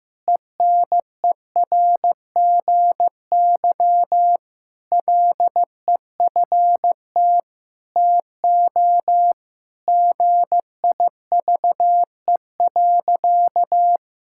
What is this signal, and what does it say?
Morse code, heard as a keyed tone, about 16 wpm, 700 Hz. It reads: ENERGY LEFT TO GIVE.
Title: Sentences from Top 1000 Words - Review Part 6 - 15wpm